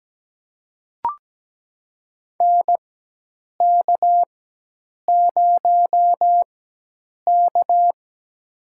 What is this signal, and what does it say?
Morse code, heard as a keyed tone, about 17 wpm, spaced as for 8 wpm, 700 Hz. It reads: NK0K
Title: USA Call Signs - 4x Character Spacing - 17wpm